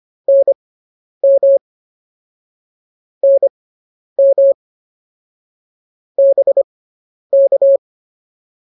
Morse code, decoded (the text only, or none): NM NM BK